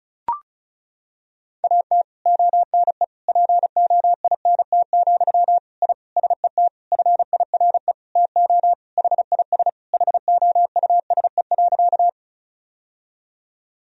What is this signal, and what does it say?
Morse code, heard as a keyed tone, about 34 wpm, 700 Hz. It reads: AT ONE POINT, I SET FIRE TO HIS HOUSE.
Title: Sentences from Top 1000 Words - Review Part 1 - 35wpm